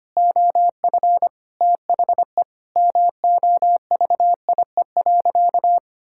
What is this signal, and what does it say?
Morse code, heard as a keyed tone, about 25 wpm, 700 Hz. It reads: OF THE MOVIE.
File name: practice / 25wpm / Sentences with 4-Word Phrases - 25wpm